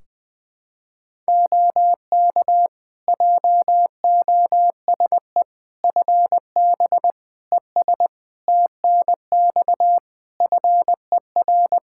Morse code, heard as a keyed tone, about 21 wpm, 700 Hz. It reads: OK JOSE FB ES TNX FER
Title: K1USN SST QSO Practice 20wpm